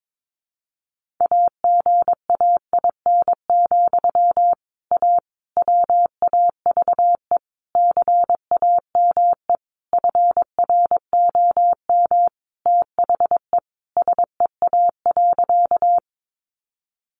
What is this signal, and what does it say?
Morse code, heard as a keyed tone, about 22 wpm, 700 Hz. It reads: AGAIN, A WAVE CAME FROM THE SEA.
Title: Sentences from Top 1400 Words 22wpm